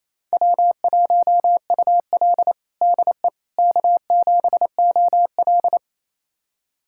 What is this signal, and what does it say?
Morse code, heard as a keyed tone, about 28 wpm, 700 Hz. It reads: W1UL DE K7OL
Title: USA Callsign DE Callsign - 4 Characters - 28wpm